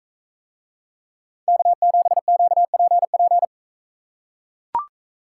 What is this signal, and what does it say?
Morse code, heard as a keyed tone, about 40 wpm, 700 Hz. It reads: K7QPP